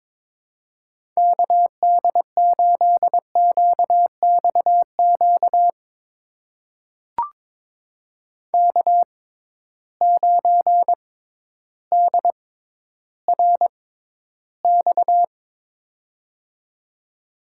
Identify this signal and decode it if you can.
Morse code, heard as a keyed tone, about 22 wpm, spaced as for 12 wpm, 700 Hz. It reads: KD8QXQ K9DRX
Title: USA Call Signs - 6x Character Spacing - 22wpm